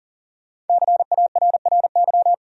Morse code, heard as a keyed tone, about 54 wpm, spaced as for 47 wpm, 700 Hz. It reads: CARRY
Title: Speed-Racing Top 500 Words 40wpm